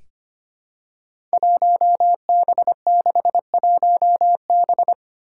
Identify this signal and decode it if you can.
Morse code, heard as a keyed tone, about 25 wpm, 700 Hz. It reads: 1B61B
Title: Five Challenging Letter-Number Combinations 25wpm